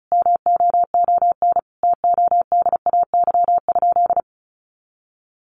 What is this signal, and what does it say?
Morse code, heard as a keyed tone, about 34 wpm, 700 Hz. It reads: MOON TODAY?
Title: Sentences from Top 1200 Words 35wpm